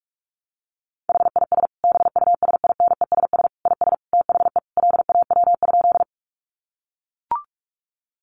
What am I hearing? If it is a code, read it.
Morse code, heard as a keyed tone, about 44 wpm, 700 Hz. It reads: HIS BUSINESS IS THE LAW?
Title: Sentences from Top 600 Words 45wpm